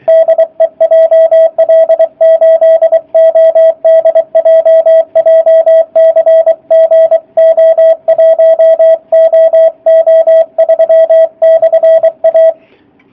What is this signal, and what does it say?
Morse code, heard as a keyed tone, about 23 wpm, 650 Hz, with radio noise. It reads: DE JL8ODJ JCG O1OO3/A
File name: jh8cbh / 2019cw / sp27- w